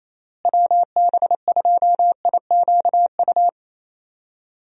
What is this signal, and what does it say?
Morse code, heard as a keyed tone, about 28 wpm, 700 Hz. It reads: WB2IQU
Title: USA Call Signs - 6 Characters - Repeated 3x - 28wpm